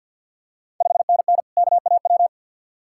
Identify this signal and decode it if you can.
Morse code, heard as a keyed tone, about 53 wpm, 700 Hz. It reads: 5NN KAW
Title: Ontario QSO Party Exchange 50wpm